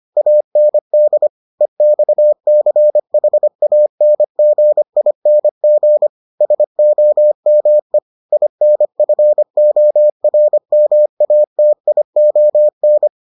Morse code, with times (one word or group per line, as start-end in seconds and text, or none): 0.2-1.3 AND
1.6-6.1 EXCHANGING
6.4-8.0 SOME
8.3-13.1 INFORMATION